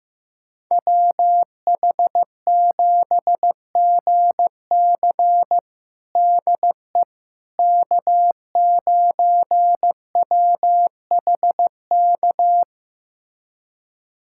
Morse code, heard as a keyed tone, about 16 wpm, 700 Hz. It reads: WH7GC DE K9WHK
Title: USA Callsign DE Callsign - 5 Characters - 15wpm